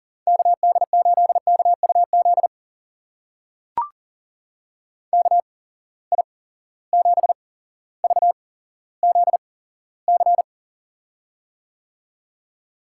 Morse code, heard as a keyed tone, about 40 wpm, spaced as for 17 wpm, 700 Hz. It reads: KD8KUZ KI7VZC
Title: USA Call Signs - 6 Characters - 8x Character Spacing - 40wpm